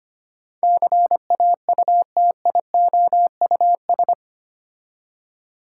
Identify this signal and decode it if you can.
Morse code, heard as a keyed tone, about 25 wpm, 700 Hz. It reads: CAUTIOUS
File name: 8 Letter Words - First 3 Letters Given - 25wpm